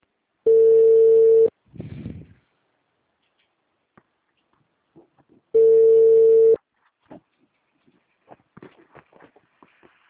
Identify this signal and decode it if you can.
Ringback tone (China, 450 Hz).
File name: out- -2001- - - .14211